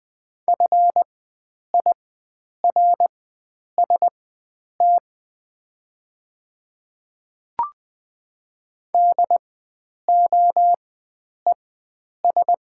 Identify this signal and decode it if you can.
Morse code, heard as a keyed tone, about 20 wpm, spaced as for 9 wpm, 700 Hz. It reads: FIRST DOES
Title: 4x Character Spacing - Top 300 Words - 20wpm